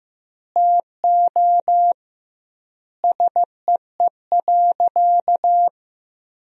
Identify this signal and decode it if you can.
Morse code, heard as a keyed tone, about 15 wpm, 700 Hz. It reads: TO SEE.
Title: 2x Word-Spacing - Sentences from Top 500 Words - 15wpm